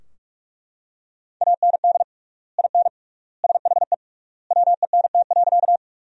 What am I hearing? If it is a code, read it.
Morse code, heard as a keyed tone, about 42 wpm, 700 Hz. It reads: AND IN SHE WENT.
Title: 3x Word-Spacing - Sentences from Top 200 Words - 45wpm